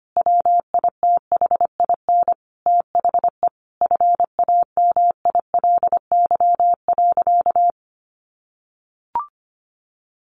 Morse code, heard as a keyed tone, about 25 wpm, 700 Hz. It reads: WITHIN THE FAMILY.
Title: Sentences from Top 1500 Words 25wpm